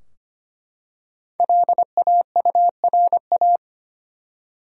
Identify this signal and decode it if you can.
Morse code, heard as a keyed tone, about 25 wpm, 700 Hz. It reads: LAURA
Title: USA Common Names - Max 5 Letters - 25wpm